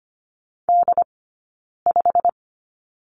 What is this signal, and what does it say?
Morse code, heard as a keyed tone, about 25 wpm, spaced as for 9 wpm, 700 Hz. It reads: D5
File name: Lesson 15d - Number 5 - Two Characters 25wpm